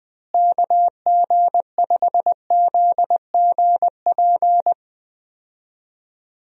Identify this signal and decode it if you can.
Morse code, heard as a keyed tone, about 20 wpm, 700 Hz. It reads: KG5ZGP
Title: USA Call Signs - 6 Characters - Repeated 3x - 20wpm